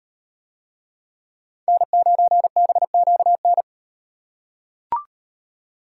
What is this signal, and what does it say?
Morse code, heard as a keyed tone, about 38 wpm, 700 Hz. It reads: N9BQN